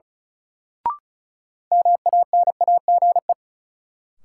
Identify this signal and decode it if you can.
Morse code, heard as a keyed tone, about 36 wpm, 700 Hz. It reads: MANAGE
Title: Sentences from Top 1400 Words - New Word Drill - 35wpm